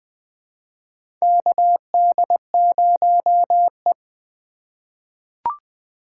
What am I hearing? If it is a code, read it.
Morse code, heard as a keyed tone, about 20 wpm, 700 Hz. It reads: KD0E
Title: USA Call Signs - 4 Characters - 3x Character Spacing - 20wpm